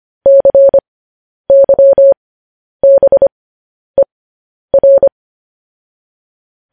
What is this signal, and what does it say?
Morse code, heard as a keyed tone, about 25 wpm, spaced as for 10 wpm, 550 Hz. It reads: CYBER
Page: news headlines in morse code at 10 wpm